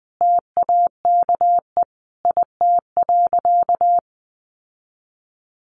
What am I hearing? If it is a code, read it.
Morse code, heard as a keyed tone, about 20 wpm, 700 Hz. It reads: TAKE IT.
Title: Sentences from Top 700 Words 20wpm